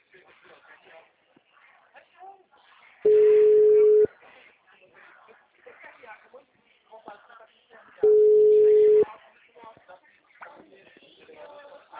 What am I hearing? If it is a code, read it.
Ringback tone (Europe, 425 Hz), with speech.
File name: out- -2005- - - .228